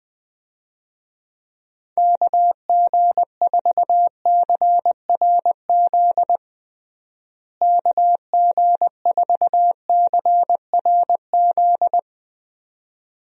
Morse code, heard as a keyed tone, about 20 wpm, 700 Hz. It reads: KG4CRZ KG4CRZ
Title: USA Call Signs - 6 Characters - Repeated 2x - 20wpm